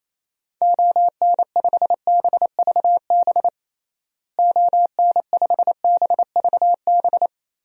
Morse code, heard as a keyed tone, about 28 wpm, 700 Hz. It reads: ON5BVB ON5BVB
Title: DXCC-100 Call Signs - 6 Characters - Repeated 3x - 28wpm